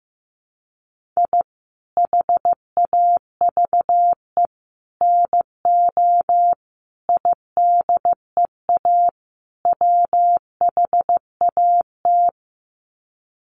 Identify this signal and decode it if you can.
Morse code, heard as a keyed tone, about 15 wpm, 700 Hz. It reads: I HAVE NO IDEA WHAT
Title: Sets of 5 Words Encore 15wpm